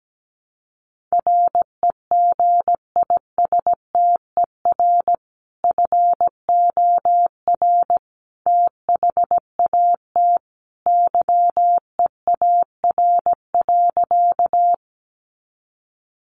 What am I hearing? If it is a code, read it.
Morse code, heard as a keyed tone, about 17 wpm, 700 Hz. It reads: REGISTER FOR THAT YEAR.
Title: Sentences from Top 1200 Words 17wpm